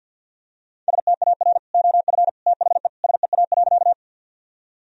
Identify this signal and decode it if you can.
Morse code, heard as a keyed tone, about 50 wpm, 700 Hz. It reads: STAR OF THE SEA.
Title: Sentences from Top 1000 Words - Review Part 2 - 50wpm